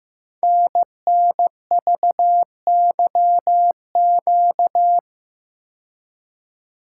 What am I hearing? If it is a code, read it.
Morse code, heard as a keyed tone, about 15 wpm, 700 Hz. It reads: NNVYQ
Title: Code Groups - 5 Letters - 15wpm